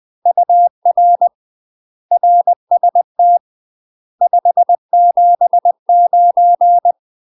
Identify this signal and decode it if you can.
Morse code, heard as a keyed tone, about 20 wpm, 700 Hz. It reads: UR RST 579
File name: UK MASTER QSOs TRAINING / 20WPM / UK MASTER QSOs 0019